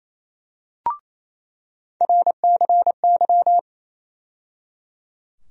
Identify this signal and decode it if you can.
Morse code, heard as a keyed tone, about 28 wpm, 700 Hz. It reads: RCY
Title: USA Call Sign Postfixes - 3 Letters 28wpm